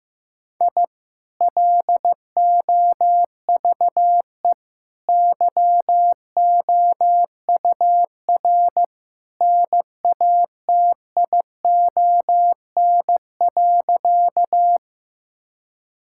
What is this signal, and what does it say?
Morse code, heard as a keyed tone, about 15 wpm, 700 Hz. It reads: I LOVE YOUR NATION.
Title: Sentences from Top 1000 Words - Review Part 6 - 15wpm